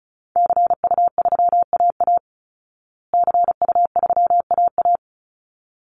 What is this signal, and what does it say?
Morse code, heard as a keyed tone, about 34 wpm, 700 Hz. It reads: CU3AA CU3AA
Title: Call Signs Repeated 3x - 35wpm